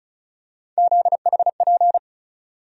Morse code, heard as a keyed tone, about 36 wpm, 700 Hz. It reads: ZHP
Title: USA Call Sign Postfixes - 3 Letters 35wpm